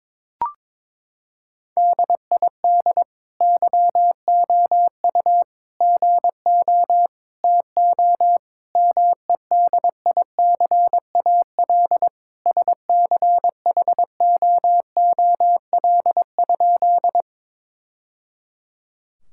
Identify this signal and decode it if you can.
Morse code, heard as a keyed tone, about 22 wpm, 700 Hz. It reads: DID YOU GO TO MEDICAL SCHOOL?
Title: Sentences from Top 1000 Words - Review Part 5 - 22wpm